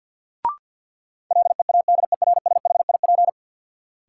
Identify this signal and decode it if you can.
Morse code, heard as a keyed tone, about 53 wpm, 700 Hz. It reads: LEADERSHIP
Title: Sentences from Top 1600 Words - New Word Drill - 50wpm